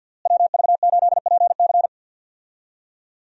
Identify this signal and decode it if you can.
Morse code, heard as a keyed tone, about 51 wpm, 700 Hz. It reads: WV8PC
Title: USA Call Signs - 5 Characters - Repeated 3x - 50wpm